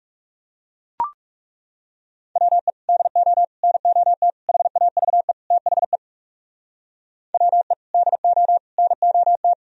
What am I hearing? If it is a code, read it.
Morse code, heard as a keyed tone, about 35 wpm, 700 Hz. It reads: WE DO NOT HAVE THE WE DO NOT
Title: Speed-Racing - Sets of 5 Words 30wpm